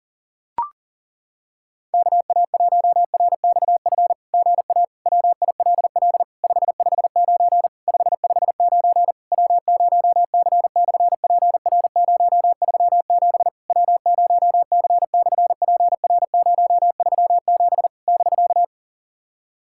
Morse code, heard as a keyed tone, about 40 wpm, 700 Hz. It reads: KA1RXF GA WILL 559 559 W0C/PR037 W0C/PR037 <BK>
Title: SOTA QSO Practice 40wpm